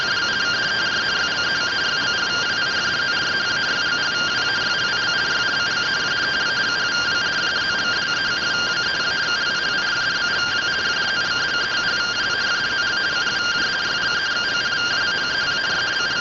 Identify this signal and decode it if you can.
Two-tone teleprinter signal, about 170 Hz shift.